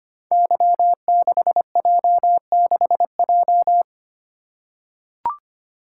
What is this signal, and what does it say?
Morse code, heard as a keyed tone, about 25 wpm, 700 Hz. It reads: Y6J6J
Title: Code Groups - 5 Letter:Numbers - 25wpm